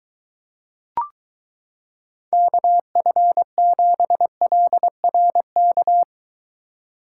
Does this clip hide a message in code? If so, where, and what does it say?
Yes: Morse code from 2.3 to 6.0 s, heard as a keyed tone, about 23 wpm, 700 Hz. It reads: KF7LRK